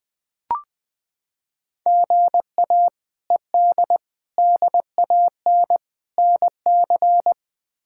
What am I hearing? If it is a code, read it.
Morse code, heard as a keyed tone, about 20 wpm, 700 Hz. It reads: GA ED DAN NC
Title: K1USN SST QSO Practice - 20wpm